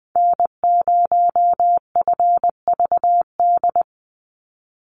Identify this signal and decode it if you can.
Morse code, heard as a keyed tone, about 20 wpm, 700 Hz. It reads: N0FVD
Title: USA Call Signs 20wpm